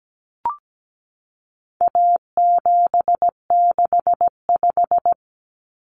Morse code, heard as a keyed tone, about 17 wpm, 700 Hz. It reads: A765
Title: Common Cut Numbers - 4-Digit 17wpm